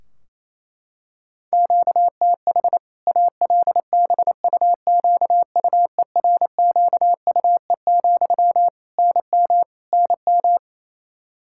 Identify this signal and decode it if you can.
Morse code, heard as a keyed tone, about 28 wpm, 700 Hz. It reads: QTH ALBUQUERQUE, NM NM
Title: Practice QSO - Exchange 1 - Beginning - 28wpm